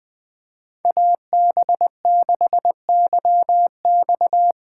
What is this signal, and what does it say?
Morse code, heard as a keyed tone, about 20 wpm, 700 Hz. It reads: AB6YX
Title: Speed Racing - USA 2x2 Call Signs 17wpm